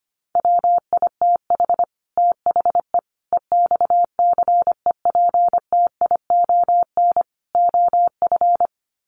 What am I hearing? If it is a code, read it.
Morse code, heard as a keyed tone, about 25 wpm, 700 Hz. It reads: WITH THE EXCEPTION OF
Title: Sets of 4 Words 25wpm